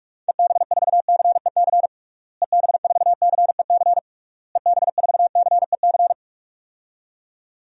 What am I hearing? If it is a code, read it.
Morse code, heard as a keyed tone, about 42 wpm, 700 Hz. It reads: EB4CEC EB4CEC EB4CEC
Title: DXCC-100 Call Signs - 6 Characters - Repeated 3x - 45wpm